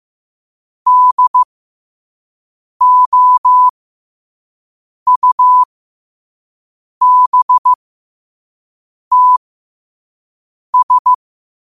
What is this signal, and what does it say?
Morse code, heard as a keyed tone, about 15 wpm, spaced as for 5 wpm, 1000 Hz. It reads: DOUBTS
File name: qotdfiles / 5wpm